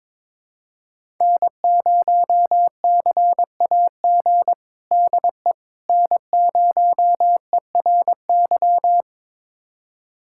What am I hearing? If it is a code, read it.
Morse code, heard as a keyed tone, about 22 wpm, 700 Hz. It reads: N0CAG DE N0ERY
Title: USA Callsign DE Callsign - 5 Characters - 22wpm